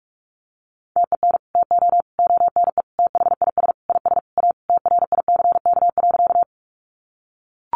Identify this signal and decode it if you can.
Morse code, heard as a keyed tone, about 44 wpm, 700 Hz. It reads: TEN TO ONE THIS IS A TRICK.
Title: Sentences from Top 1500 Words 45wpm